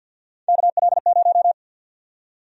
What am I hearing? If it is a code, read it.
Morse code, heard as a keyed tone, about 47 wpm, 700 Hz. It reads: KL0
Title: USA Call Sign Prefixes - Repeated 2x - 50wpm